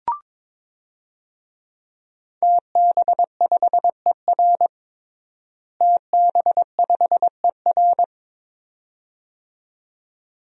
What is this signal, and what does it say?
Morse code, heard as a keyed tone, about 22 wpm, 700 Hz. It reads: TB5ER TB5ER
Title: DXCC-100 Call Signs - 5 Characters - Repeated 2x - 22wpm